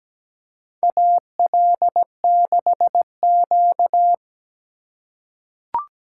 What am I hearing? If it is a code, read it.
Morse code, heard as a keyed tone, about 17 wpm, 700 Hz. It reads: AL6Q
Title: USA Call Signs - 4 Characters - 2x Character Spacing - 17wpm